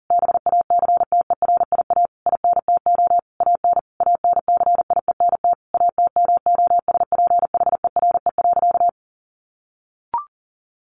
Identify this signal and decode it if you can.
Morse code, heard as a keyed tone, about 40 wpm, 700 Hz. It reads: BACTERIA INTO AN ANCIENT ATMOSPHERE.
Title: A Sound Of Thunder 40wpm